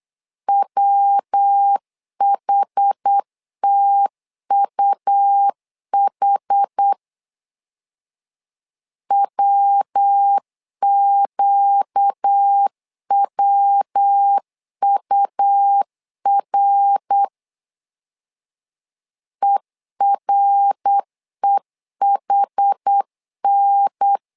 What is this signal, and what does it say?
Morse code, heard as a keyed tone, about 8 wpm, 800 Hz. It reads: WHTUH WQWUR EREHN
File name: LEZIONI / 4 Q 8wpm 2732